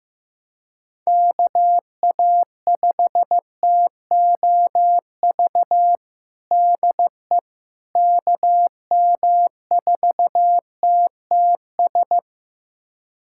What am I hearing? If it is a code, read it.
Morse code, heard as a keyed tone, about 15 wpm, 700 Hz. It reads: KA5TOV DE KM4TTS
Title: USA Callsign DE Callsign - 6 Characters - 15wpm